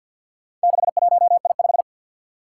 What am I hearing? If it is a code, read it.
Morse code, heard as a keyed tone, about 48 wpm, 700 Hz. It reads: 61I5
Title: Four Challenging Letter-Number Combinations 50wpm